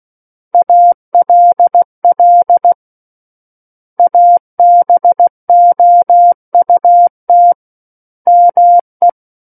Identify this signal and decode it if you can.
Morse code, heard as a keyed tone, about 16 wpm, 700 Hz. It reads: ALL ABOUT ME
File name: Book One / 07-Chapter 7 - 16 WPM